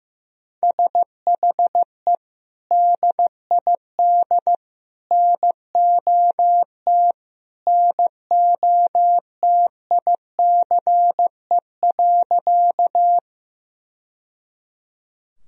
Morse code, heard as a keyed tone, about 15 wpm, 700 Hz. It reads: SHE DID NOT NOTICE.
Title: Sentences from Top 1000 Words - Review Part 2 - 15wpm